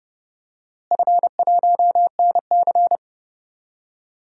Morse code, heard as a keyed tone, about 30 wpm, 700 Hz. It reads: F1NC
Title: DXCC-100 Call Signs - 4 Characters - Repeated 2x - 30wpm